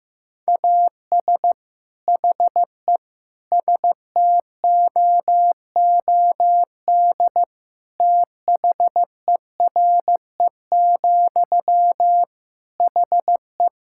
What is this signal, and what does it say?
Morse code, heard as a keyed tone, about 15 wpm, 700 Hz. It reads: AS HE STOOD THERE, HE
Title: Sentences from Top 1600 Words 15wpm